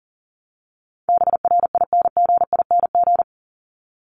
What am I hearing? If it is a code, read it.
Morse code, heard as a keyed tone, about 38 wpm, 700 Hz. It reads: BRINGING
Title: ING Suffix - Top 100 Words with Base Words 40wpm